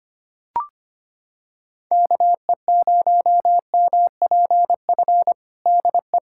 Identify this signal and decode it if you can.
Morse code, heard as a keyed tone, about 25 wpm, 700 Hz. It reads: KE0MPF DE
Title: Practice QSO - Exchange 4 - Ending - 25wpm